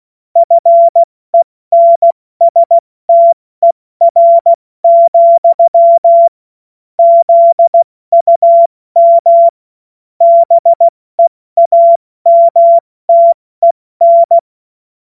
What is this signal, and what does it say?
Morse code, heard as a keyed tone, about 16 wpm, 680 Hz. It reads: FENSTER, ZUM BEAMTEN
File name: Witz-02-September-2025-70-F80